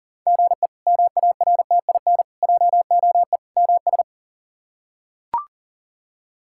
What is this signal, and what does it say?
Morse code, heard as a keyed tone, about 40 wpm, 700 Hz. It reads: GE MARTIN JOE MS
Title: K1USN SST QSO Practice - 40wpm